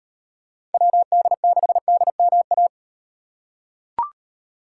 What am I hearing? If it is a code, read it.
Morse code, heard as a keyed tone, about 36 wpm, 700 Hz. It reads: WD6DMA